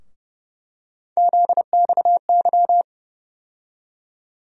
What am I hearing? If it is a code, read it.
Morse code, heard as a keyed tone, about 30 wpm, 700 Hz. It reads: ZXY
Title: USA Call Sign Postfixes - 3 Letters 30wpm